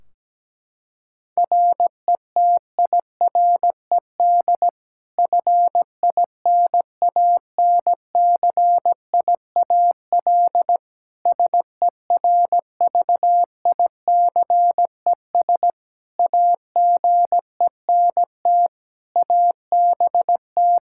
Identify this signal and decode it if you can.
Morse code, heard as a keyed tone, about 17 wpm, 700 Hz. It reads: RETIRED FINANCIAL SERVICES AGENT ABT
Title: K1USN SST QSO Practice 17wpm